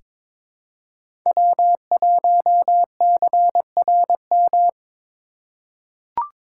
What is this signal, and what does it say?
Morse code, heard as a keyed tone, about 22 wpm, 700 Hz. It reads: W1CRM
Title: USA Call Signs - 5 Characters - 3x Character Spacing - 22wpm